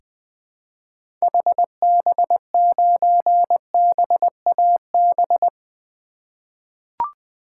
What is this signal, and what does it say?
Morse code, heard as a keyed tone, about 20 wpm, 700 Hz. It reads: HB9BAB
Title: 2023 SOTA Activator Call Signs - 20wpm